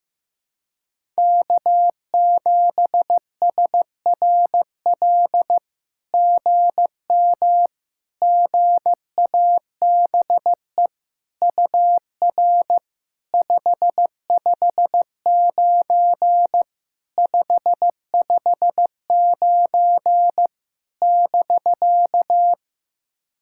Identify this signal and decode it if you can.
Morse code, heard as a keyed tone, about 15 wpm, 700 Hz. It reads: K7SRL GM GABE UR 559 559 <BK>
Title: SOTA QSO Practice 15wpm